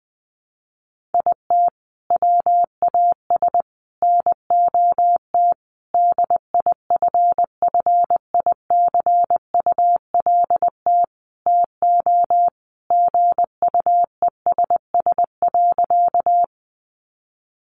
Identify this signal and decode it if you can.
Morse code, heard as a keyed tone, about 20 wpm, 700 Hz. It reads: IT WAS NOT DIFFICULT TO GUESS.
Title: Sentences from Top 1500 Words 20wpm